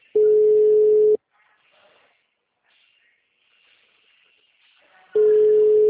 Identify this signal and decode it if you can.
Ringback tone (Europe, 425 Hz).